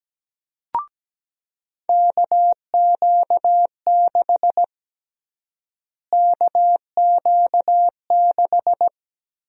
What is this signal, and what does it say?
Morse code, heard as a keyed tone, about 17 wpm, 700 Hz. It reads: KQ6 KQ6
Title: USA Call Sign Prefixes - Repeated 2x - 17wpm